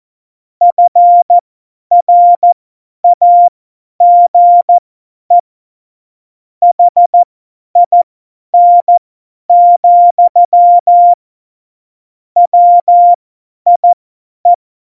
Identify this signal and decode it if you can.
Morse code, heard as a keyed tone, about 14 wpm, spaced as for 10 wpm, 700 Hz. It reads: FRAGE HIN, WIE
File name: Morsekurs / Witze / Witz 29. September 2020, 50 F70